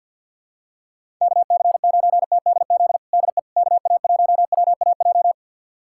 Morse code, heard as a keyed tone, about 49 wpm, 700 Hz. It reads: KC9TDZ DE KA1WAJ
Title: USA Callsign DE Callsign - 6 Characters - 50wpm